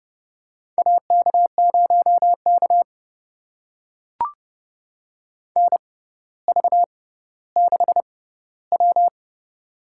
Morse code, heard as a keyed tone, about 30 wpm, spaced as for 15 wpm, 700 Hz. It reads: AK0K NV6W
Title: USA Call Signs - 4 Characters - 6x Character Spacing - 30wpm